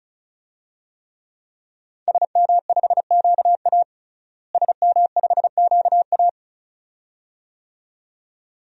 Morse code, heard as a keyed tone, about 36 wpm, 700 Hz. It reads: SM5QA SM5QA
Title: DXCC-100 Call Signs - 5 Characters - Repeated 2x - 35wpm